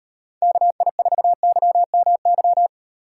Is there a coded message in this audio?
Yes: Morse code from 0.4 to 2.7 s, heard as a keyed tone, about 38 wpm, 700 Hz. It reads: KI4YMY